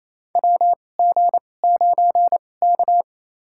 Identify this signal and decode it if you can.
Morse code, heard as a keyed tone, about 28 wpm, spaced as for 20 wpm, 700 Hz. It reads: WG9K
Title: USA Call Signs - 4 Characters - 2x Character Spacing - 28wpm